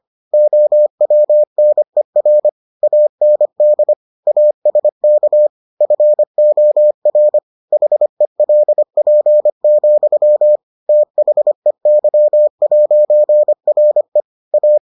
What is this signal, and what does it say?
Morse code, heard as a keyed tone, about 25 wpm, 600 Hz. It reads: OWNER AND ASK FOR HELP, THEY'RE A